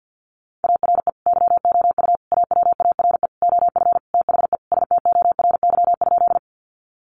Morse code, heard as a keyed tone, about 50 wpm, 700 Hz. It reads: ARE YOU AWARE OF THE STORY?
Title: Sentences from Top 1200 Words 50wpm